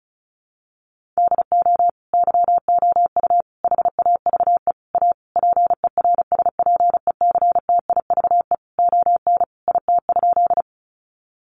Morse code, heard as a keyed tone, about 34 wpm, 700 Hz. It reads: DO YOU HAVE A PERSPECTIVE ON IT?
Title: Sentences from Top 1200 Words 35wpm